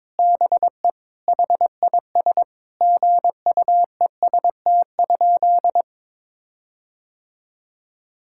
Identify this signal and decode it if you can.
Morse code, heard as a keyed tone, about 22 wpm, 700 Hz. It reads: BE HIS GUEST?
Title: Sentences from Top 1600 Words 22wpm